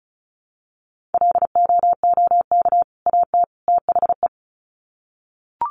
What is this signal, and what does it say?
Morse code, heard as a keyed tone, about 34 wpm, 700 Hz. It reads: LOOK AT THE
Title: Sets of 3 Words 35wpm